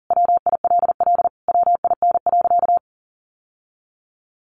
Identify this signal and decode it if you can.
Morse code, heard as a keyed tone, about 40 wpm, 700 Hz. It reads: WILL WIN.
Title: Sentences from Top 1500 Words 40wpm